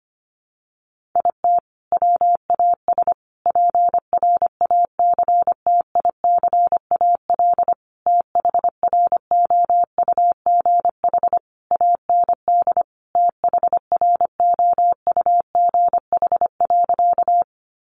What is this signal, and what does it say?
Morse code, heard as a keyed tone, about 25 wpm, 700 Hz. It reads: IT WAS PRACTICAL THROUGH AND THROUGH.
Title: Sentences from Top 1400 Words 25wpm